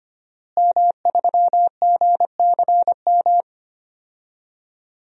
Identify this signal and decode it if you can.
Morse code, heard as a keyed tone, about 25 wpm, 700 Hz. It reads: M3GCM
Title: DXCC-100 Call Signs - 5 Characters - Repeated 2x - 25wpm